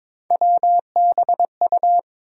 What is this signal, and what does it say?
Morse code, heard as a keyed tone, about 21 wpm, 700 Hz. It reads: WBU
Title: USA Call Sign Postfixes - 3 Letters 22wpm